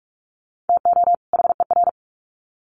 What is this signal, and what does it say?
Morse code, heard as a keyed tone, about 44 wpm, 700 Hz. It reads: TO HER
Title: Sets of 2 Words 45wpm